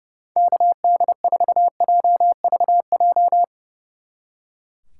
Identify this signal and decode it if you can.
Morse code, heard as a keyed tone, about 38 wpm, 700 Hz. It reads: KD4JVJ